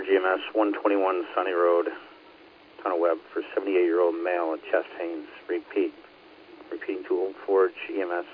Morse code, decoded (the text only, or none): none